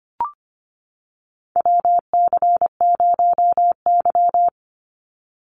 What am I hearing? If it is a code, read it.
Morse code, heard as a keyed tone, about 25 wpm, 700 Hz. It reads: WC0Y
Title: NA SOTA Chaser Callsigns 25wpm